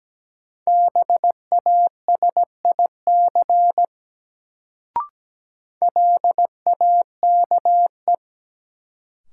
Morse code, heard as a keyed tone, about 17 wpm, 700 Hz. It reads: BASIC LAKE
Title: Sentences from Top 1000 Words - Review Part 3 - New Word Drill - 17wpm